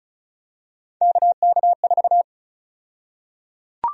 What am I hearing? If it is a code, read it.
Morse code, heard as a keyed tone, about 36 wpm, 700 Hz. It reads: KK4
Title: USA Call Sign Prefixes - Repeated 2x - 35wpm